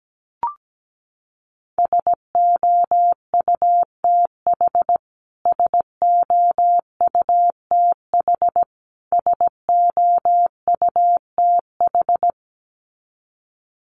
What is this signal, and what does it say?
Morse code, heard as a keyed tone, about 17 wpm, 700 Hz. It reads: SOUTH SOUTH SOUTH
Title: Top 500 Words Repeated 17wpm